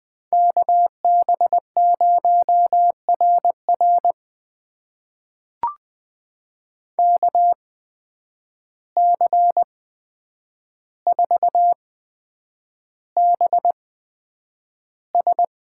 Morse code, heard as a keyed tone, about 20 wpm, spaced as for 9 wpm, 700 Hz. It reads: KB0RR KC4BS
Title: USA Call Signs - 8x Character Spacing - 20wpm